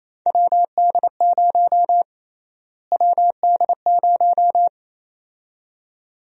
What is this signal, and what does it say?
Morse code, heard as a keyed tone, about 28 wpm, 700 Hz. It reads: WD0 WD0
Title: USA Call Sign Prefixes - Repeated 2x - 28wpm